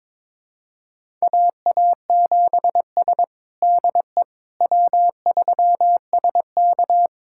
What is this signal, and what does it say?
Morse code, heard as a keyed tone, about 22 wpm, 700 Hz. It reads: AA7S DE W3SK
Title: USA Callsign DE Callsign - 4 Characters - 22wpm